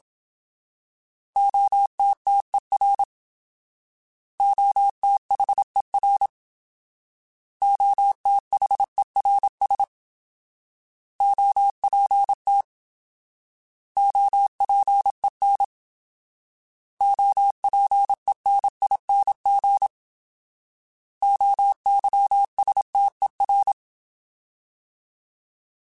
Morse code, heard as a keyed tone, about 27 wpm, 800 Hz. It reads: OTTER OTHER OTHERS OPT OPEN OPENING OYSTER